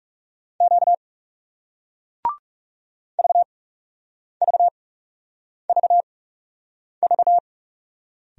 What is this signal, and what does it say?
Morse code, heard as a keyed tone, about 40 wpm, spaced as for 8 wpm, 700 Hz. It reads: Q VVVV